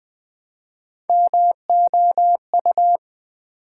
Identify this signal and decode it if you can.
Morse code, heard as a keyed tone, about 20 wpm, 700 Hz. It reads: MOU
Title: Build A Word - 1 Letter At A Time - 5-Letter Words - 20wpm